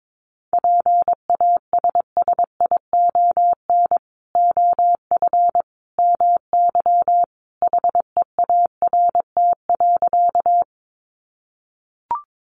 Morse code, heard as a keyed tone, about 22 wpm, 700 Hz. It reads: PASSION OF MY HEART.
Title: Sentences from Top 1300 Words 22wpm